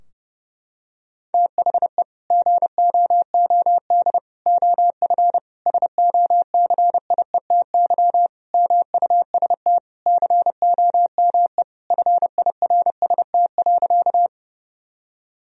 Morse code, heard as a keyed tone, about 30 wpm, 700 Hz. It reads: THE GOOD OF SOCIETY MUST COME FIRST.
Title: Sentences from Top 1000 Words - Review Part 4 - 30wpm